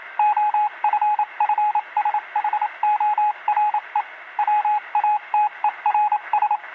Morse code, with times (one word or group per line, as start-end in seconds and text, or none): 0.2-4.0 OFFSHORE
4.4-6.6 WATERS